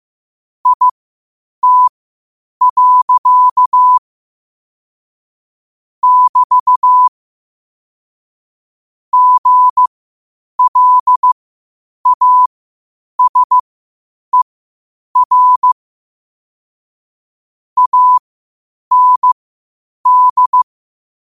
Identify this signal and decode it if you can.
Morse code, heard as a keyed tone, about 15 wpm, spaced as for 8 wpm, 1000 Hz. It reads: IT. <BT> GLASER AND
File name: qotdfiles / 8wpm